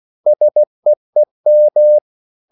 Morse code, heard as a keyed tone, about 16 wpm, 600 Hz. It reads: SEEM
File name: CW / Practice / EnglishQuotations0116